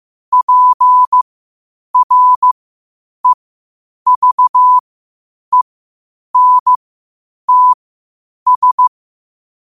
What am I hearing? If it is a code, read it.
Morse code, heard as a keyed tone, about 15 wpm, spaced as for 8 wpm, 1000 Hz. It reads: PREVENTS
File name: qotdfiles / 8wpm